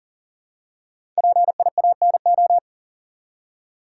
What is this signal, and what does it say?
Morse code, heard as a keyed tone, about 39 wpm, 700 Hz. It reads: PIANO
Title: Five Letter Words - 40wpm